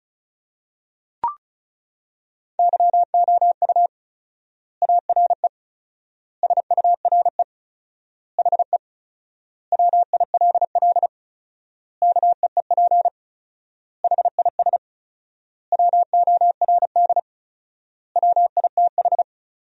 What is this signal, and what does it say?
Morse code, heard as a keyed tone, about 36 wpm, 700 Hz. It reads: YOU ARE SURE HE WILL KEEP HIS WORD WITH
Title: 4x Word-Spacing - Sentences from Top 300 Words - 35wpm